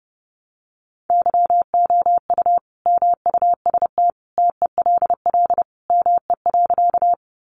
Morse code, heard as a keyed tone, about 30 wpm, 700 Hz. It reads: YOU MUST TELL ME.
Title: Sentences from Top 200 Words 30wpm